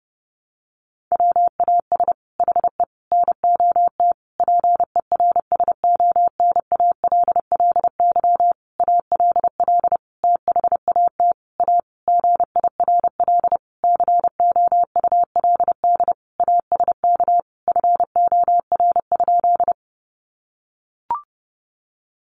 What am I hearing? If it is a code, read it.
Morse code, heard as a keyed tone, about 30 wpm, 700 Hz. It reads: WAS HE NOT PERSONALLY ALL THAT A GIRL COULD ASK FOR?
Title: Sentences from Top 1400 Words 30wpm